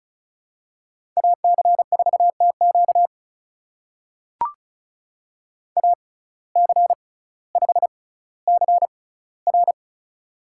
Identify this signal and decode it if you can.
Morse code, heard as a keyed tone, about 34 wpm, spaced as for 18 wpm, 700 Hz. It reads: AC4TQ AC5CR
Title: USA Call Signs - 5 Characters - 6x Character Spacing - 35wpm